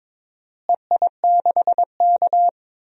Morse code, heard as a keyed tone, about 22 wpm, 700 Hz. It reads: EI6K
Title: DXCC-100 Call Signs - 4 Characters - Repeated 2x - 22wpm